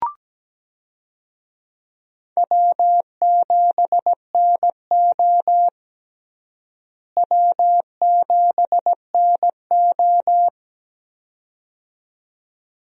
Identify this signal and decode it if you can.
Morse code, heard as a keyed tone, about 17 wpm, 700 Hz. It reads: W7NO W7NO
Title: USA Call Signs - 4 Characters - Repeated 2x - 17wpm